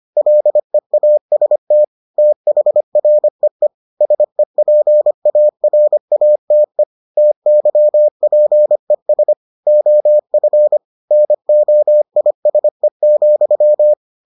Morse code, heard as a keyed tone, about 25 wpm, 600 Hz. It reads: LEAST THREE SEPARATE TYPES OF NOISE,